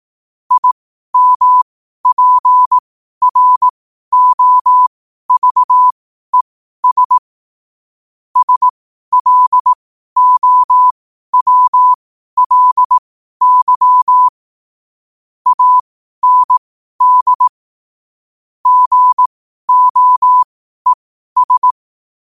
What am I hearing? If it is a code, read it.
Morse code, heard as a keyed tone, about 18 wpm, spaced as for 12 wpm, 1000 Hz. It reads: IMPROVES SLOWLY AND GOES
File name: qotdfiles / 12wpm